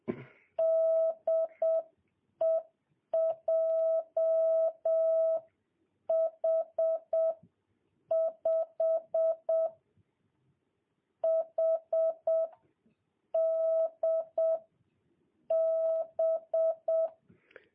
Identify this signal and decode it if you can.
Morse code, heard as a keyed tone, about 7 wpm, spaced as for 5 wpm, 650 Hz. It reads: DE JH5HDB
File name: jh8cbh / 2022cw